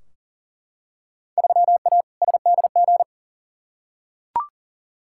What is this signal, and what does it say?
Morse code, heard as a keyed tone, about 40 wpm, 700 Hz. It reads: 3A SDG
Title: Field Day Exchanges - 40wpm